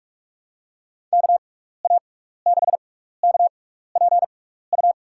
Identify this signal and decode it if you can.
Morse code, heard as a keyed tone, about 47 wpm, spaced as for 16 wpm, 700 Hz. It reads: KA6KPU
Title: USA Call Signs - 6 Characters - 6x Character Spacing - 45wpm